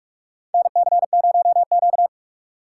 Morse code, heard as a keyed tone, about 46 wpm, 700 Hz. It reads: NC0Q
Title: USA Call Signs - 4 Characters - 3x Character Spacing - 45wpm